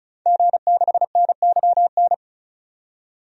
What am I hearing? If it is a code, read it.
Morse code, heard as a keyed tone, about 34 wpm, 700 Hz. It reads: G6NYN
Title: DXCC-100 Call Signs - 5 Characters - 35wpm